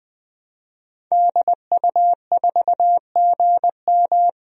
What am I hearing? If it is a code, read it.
Morse code, heard as a keyed tone, about 20 wpm, 700 Hz. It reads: DU4GM
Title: DXCC-100 Call Signs - 5 Characters - Repeated 3x - 20wpm